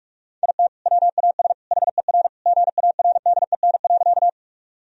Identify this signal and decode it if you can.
Morse code, heard as a keyed tone, about 45 wpm, 700 Hz. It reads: IT WAS HER GARDEN.
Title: Sentences from Top 1000 Words - Review Part 4 - 45wpm